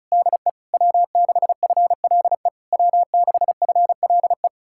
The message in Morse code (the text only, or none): DE W6FLE W6FLE